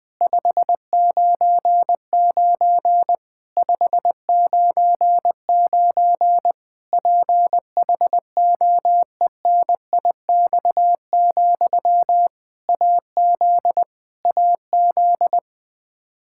Morse code, heard as a keyed tone, about 20 wpm, 700 Hz. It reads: 599 599 PHOENIX, AZ AZ
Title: K1USN SST QSO Practice 20wpm